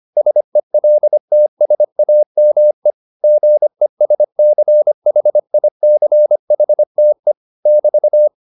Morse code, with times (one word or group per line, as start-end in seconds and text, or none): 0.2-2.9 SELTSAME
3.2-7.3 GESCHICHTE
7.7-8.3 <BT>